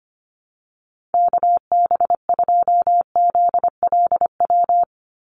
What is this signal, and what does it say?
Morse code, heard as a keyed tone, about 25 wpm, 700 Hz. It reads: KB2ZLW
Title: USA Call Signs 25wpm